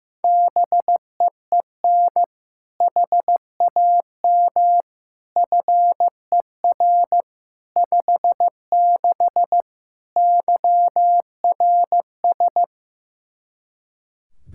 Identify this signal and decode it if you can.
Morse code, heard as a keyed tone, about 16 wpm, 700 Hz. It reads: BEEN HAM FER 56 YRS
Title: Practice QSO - Exchange 3 - More Info - 15wpm